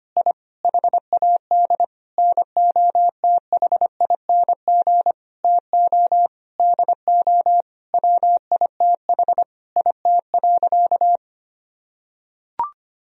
Morse code, heard as a keyed tone, about 25 wpm, 700 Hz. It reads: I HAD NOTHING TO DO WITH IT.
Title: Sentences from Top 1000 Words - Review Part 1 - 25wpm